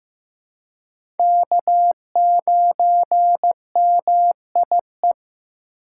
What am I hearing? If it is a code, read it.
Morse code, heard as a keyed tone, about 15 wpm, 700 Hz. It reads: K9MIE
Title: USA Call Signs - 5 Characters - 3x Character Spacing - 15wpm